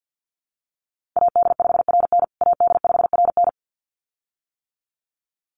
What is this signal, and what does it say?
Morse code, heard as a keyed tone, about 50 wpm, 700 Hz. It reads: AD5RN AD5RN
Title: USA Call Signs Repeated 2x 50wpm